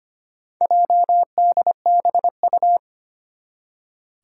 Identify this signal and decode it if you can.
Morse code, heard as a keyed tone, about 25 wpm, 700 Hz. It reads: JDBU
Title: Four Challenging Letter-Number Combinations 25wpm